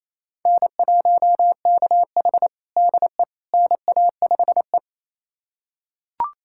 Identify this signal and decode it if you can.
Morse code, heard as a keyed tone, about 28 wpm, 700 Hz. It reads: N1KH DE NA5E
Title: USA Callsign DE Callsign - 4 Characters - 28wpm